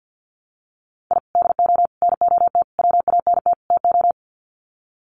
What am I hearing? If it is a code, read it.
Morse code, heard as a keyed tone, about 50 wpm, 700 Hz. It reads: I DO NOT WANT TO
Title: Sets of 5 Words Encore2 50wpm